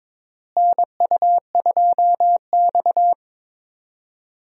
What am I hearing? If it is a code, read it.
Morse code, heard as a keyed tone, about 22 wpm, 700 Hz. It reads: NU2X
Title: USA Call Signs - 4 Characters - Repeated 3x - 22wpm